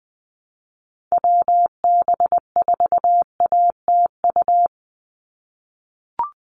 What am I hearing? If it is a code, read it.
Morse code, heard as a keyed tone, about 20 wpm, 700 Hz. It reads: WB4ATU
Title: USA Call Signs 20wpm